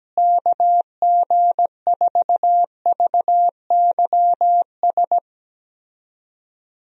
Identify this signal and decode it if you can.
Morse code, heard as a keyed tone, about 17 wpm, 700 Hz. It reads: KG4VYS
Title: USA Call Signs - 6 Characters - Repeated 2x - 17wpm